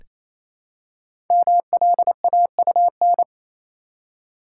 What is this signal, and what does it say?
Morse code, heard as a keyed tone, about 29 wpm, 700 Hz. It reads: MLAUN
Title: Code Groups - 5 Letters - 28wpm